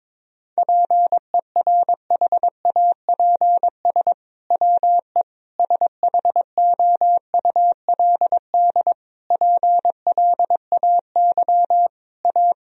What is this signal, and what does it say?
Morse code, heard as a keyed tone, about 22 wpm, 700 Hz. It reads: PERHAPS WE SHOULD PLAY A
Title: Sentences from Top 1000 Words - Review Part 3 - 22wpm